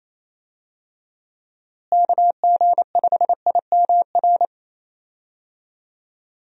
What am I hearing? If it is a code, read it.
Morse code, heard as a keyed tone, about 27 wpm, 700 Hz. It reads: KG5IMR
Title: USA Call Signs - 6 Characters - Repeated 2x - 28wpm